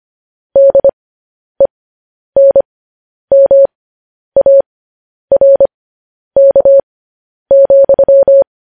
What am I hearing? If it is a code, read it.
Morse code, heard as a keyed tone, about 25 wpm, spaced as for 10 wpm, 550 Hz. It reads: DENMARK,
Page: news-headlines-in-morse-code-at-10-wpm